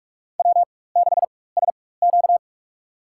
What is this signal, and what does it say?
Morse code, heard as a keyed tone, about 47 wpm, spaced as for 21 wpm, 700 Hz. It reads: W6SQ
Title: USA Call Signs - 4 Characters - 4x Character Spacing - 45wpm